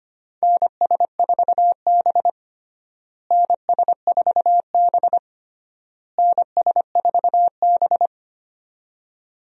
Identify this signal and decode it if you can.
Morse code, heard as a keyed tone, about 25 wpm, 700 Hz. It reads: NS4B NS4B NS4B
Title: USA Call Signs - 4 Characters - Repeated 3x - 25wpm